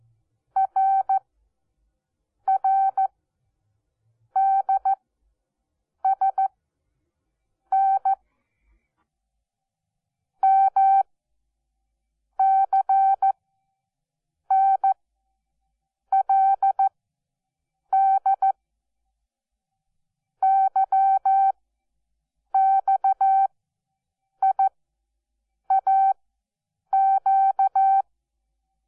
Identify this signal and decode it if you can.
Morse code, heard as a keyed tone, about 14 wpm, spaced as for 6 wpm, 780 Hz. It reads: RRDSN MCNLD YXIAQ
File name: Onlinelehrgang / morsen / DJ4UF Morsen12